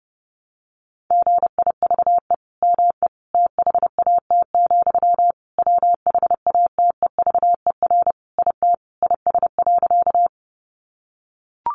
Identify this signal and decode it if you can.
Morse code, heard as a keyed tone, about 30 wpm, 700 Hz. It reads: GIVE ME THAT, WHATEVER IT IS.
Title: Sentences from Top 1100 Words 30wpm